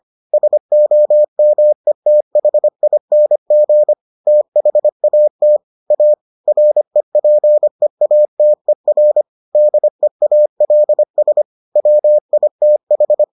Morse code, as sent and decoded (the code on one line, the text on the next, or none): ... --- -- . - .... .. -. --. / - .... .- - / .- / .-. .--. .- - . .-. / -.. .- .-.. ... / .-- .. - ....
SOMETHING THAT A REPEATER DEALS WITH